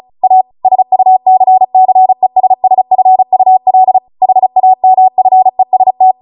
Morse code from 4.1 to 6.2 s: HAMFEST